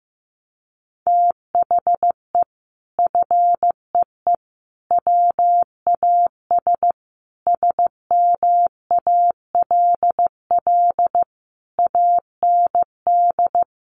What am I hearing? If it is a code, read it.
Morse code, heard as a keyed tone, about 15 wpm, 700 Hz. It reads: THE FEE WAS SMALL AND
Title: Sentences from Top 1500 Words 15wpm